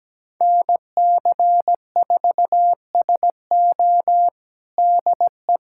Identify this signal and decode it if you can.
Morse code, heard as a keyed tone, about 17 wpm, 700 Hz. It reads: NC4SO DE
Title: USA Callsign DE Callsign - 5 Characters - 17wpm